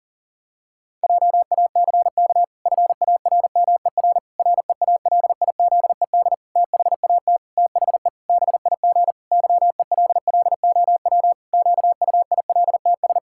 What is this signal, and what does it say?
Morse code, heard as a keyed tone, about 37 wpm, 700 Hz. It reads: JACK FARMER REALIZED THAT THE BIG YELLOW QUILTS
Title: Pangram Challenge v2 - 40wpm